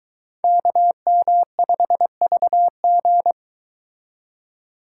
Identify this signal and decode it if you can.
Morse code, heard as a keyed tone, about 22 wpm, 700 Hz. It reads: KM5VG